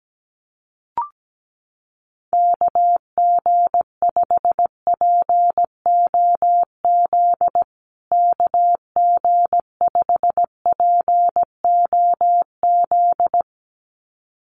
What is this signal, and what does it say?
Morse code, heard as a keyed tone, about 17 wpm, 700 Hz. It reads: KG5POZ KG5POZ
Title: USA Call Signs Repeated 2x 17wpm